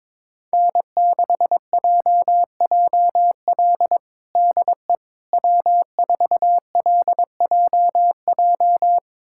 Morse code, heard as a keyed tone, about 22 wpm, 700 Hz. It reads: N6JJL DE W4LJJ
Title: USA Callsign DE Callsign - 5 Characters - 22wpm